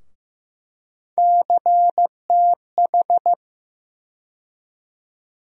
Morse code, heard as a keyed tone, about 15 wpm, 700 Hz. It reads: CTH
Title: USA Call Sign Postfixes - 3 Letters 15wpm